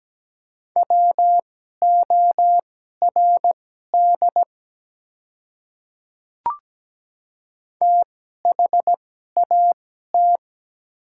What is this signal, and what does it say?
Morse code, heard as a keyed tone, about 17 wpm, spaced as for 12 wpm, 700 Hz. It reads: WORD THAT
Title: 2x Character Spacing - Top 100 Words - 17wpm